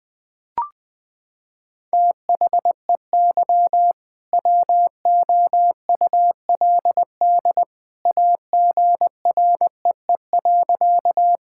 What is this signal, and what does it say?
Morse code, heard as a keyed tone, about 21 wpm, 700 Hz. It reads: THEY WOULD AGREE.
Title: Sentences with 2-Word Phrases - 20wpm